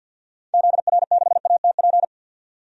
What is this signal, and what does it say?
Morse code, heard as a keyed tone, about 53 wpm, 700 Hz. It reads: ZR6ATP
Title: DXCC-100 Call Signs - 6 Characters - 50wpm